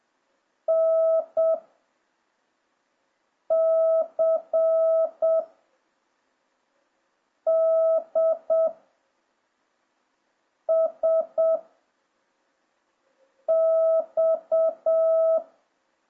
Morse code, heard as a keyed tone, about 7 wpm, 660 Hz. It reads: N C D S X